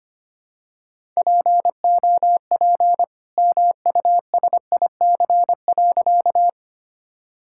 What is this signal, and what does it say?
Morse code, heard as a keyed tone, about 25 wpm, 700 Hz. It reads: POP MUSIC.
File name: practice / 25wpm / All-Age Appropriate Jokes - 25wpm